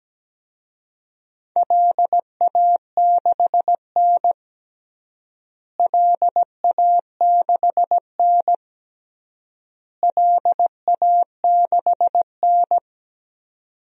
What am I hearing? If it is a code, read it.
Morse code, heard as a keyed tone, about 17 wpm, 700 Hz. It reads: LA6N LA6N LA6N
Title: DXCC-100 Call Signs - 4 Characters - Repeated 3x - 17wpm